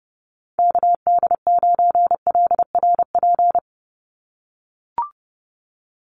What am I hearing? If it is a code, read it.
Morse code, heard as a keyed tone, about 30 wpm, 700 Hz. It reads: KD9LRP
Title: USA Call Signs 30wpm